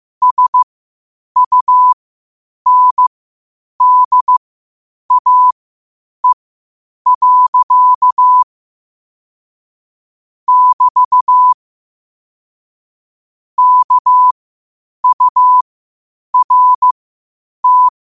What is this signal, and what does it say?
Morse code, heard as a keyed tone, about 15 wpm, spaced as for 8 wpm, 1000 Hz. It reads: SUNDAE. <BT> KURT
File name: qotdfiles / 8wpm